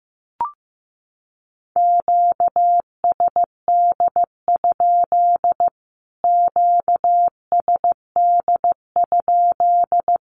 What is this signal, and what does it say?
Morse code, heard as a keyed tone, about 15 wpm, 700 Hz. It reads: QSD? QSD?
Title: Q-Codes Repeated 3x 15wpm